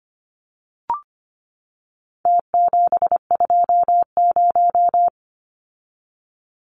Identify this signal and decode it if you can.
Morse code, heard as a keyed tone, about 25 wpm, 700 Hz. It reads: T720
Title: Numbers - Contest Style - 4-Digit 25wpm